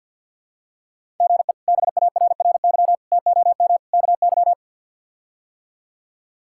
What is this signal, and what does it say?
Morse code, heard as a keyed tone, about 51 wpm, 700 Hz. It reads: GE BARRY TOM KY
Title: K1USN SST QSO Practice - 50wpm